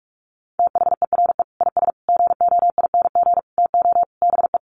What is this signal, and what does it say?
Morse code, heard as a keyed tone, about 46 wpm, 700 Hz. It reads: THERE IS GOING TO BE
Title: Sets of 5 Words Encore 45wpm